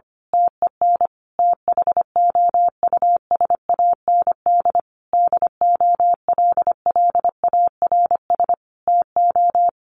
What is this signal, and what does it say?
Morse code, heard as a keyed tone, about 25 wpm, 700 Hz. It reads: TEN THOUSAND DOLLARS TO
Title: A Sound Of Thunder 25wpm